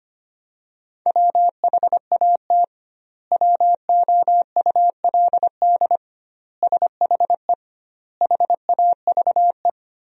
Morse code, heard as a keyed tone, about 25 wpm, 700 Hz. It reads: WHAT WOULD SHE HAVE
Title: 2x Word-Spacing - Sentences from Top 100 Words - 25wpm